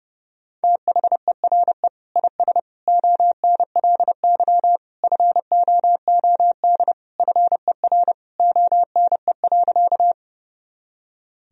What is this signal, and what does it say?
Morse code, heard as a keyed tone, about 32 wpm, 700 Hz. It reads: THERE IS ONLY FOOD FER ONE.
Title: Sentences from Top 300 Words and Abbreviations 30wpm